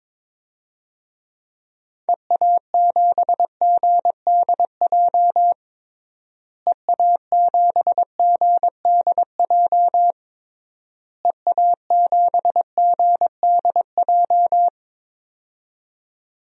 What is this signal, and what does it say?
Morse code, heard as a keyed tone, about 22 wpm, 700 Hz. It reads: EA7GDJ EA7GDJ EA7GDJ
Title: DXCC-100 Call Signs - 6 Characters - Repeated 3x - 22wpm